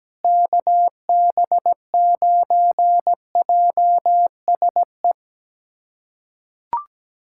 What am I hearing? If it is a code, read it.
Morse code, heard as a keyed tone, about 17 wpm, 700 Hz. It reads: KB9JSE
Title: USA Callsign DE Callsign 17wpm